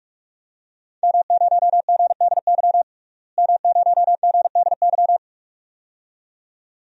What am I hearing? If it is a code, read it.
Morse code, heard as a keyed tone, about 46 wpm, 700 Hz. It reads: M0GDY M0GDY
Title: DXCC-100 Call Signs - 5 Characters - Repeated 2x - 45wpm